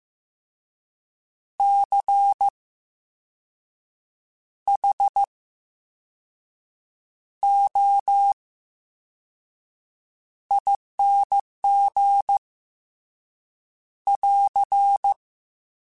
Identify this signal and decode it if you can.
Morse code, heard as a keyed tone, about 15 wpm, spaced as for 5 wpm, 800 Hz. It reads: CHOING <AR>
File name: Special INGs / Special INGs-mp3s / Slow ing @13